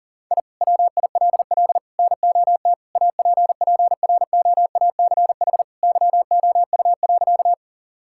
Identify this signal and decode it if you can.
Morse code, heard as a keyed tone, about 40 wpm, 690 Hz. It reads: I WILL NOT APPROACH YOU.
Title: Sentences from Top 1000 Words - Review Part 6 - 40wpm